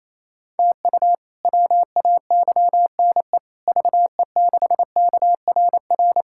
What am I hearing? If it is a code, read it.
Morse code, heard as a keyed tone, about 28 wpm, 700 Hz. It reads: TU WAYNE VE6KRR
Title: K1USN SST QSO Practice - 28wpm